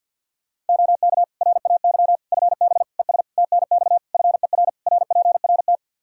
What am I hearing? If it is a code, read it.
Morse code, heard as a keyed tone, about 49 wpm, 700 Hz. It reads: OK RAY FB ES TNX FER RPRT
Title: Practice QSO - Exchange 2 - Info - 50wpm